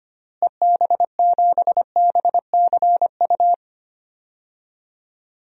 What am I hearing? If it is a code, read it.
Morse code, heard as a keyed tone, about 25 wpm, 700 Hz. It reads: EB7BCU
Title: DXCC-100 Call Signs - 6 Characters - Repeated 3x - 25wpm